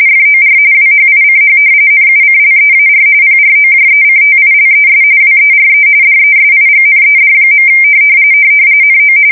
RTTY, about 45.45 baud. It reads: FOR WHICH NUCLEAR PROJECTILES WERE DESIGNED. NO
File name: fsk170 / battleshipnj